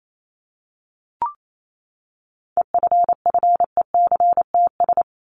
Morse code, heard as a keyed tone, about 28 wpm, 700 Hz. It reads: EFFECTS
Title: ES:S Suffix - Top 100 Words with Base Words 28wpm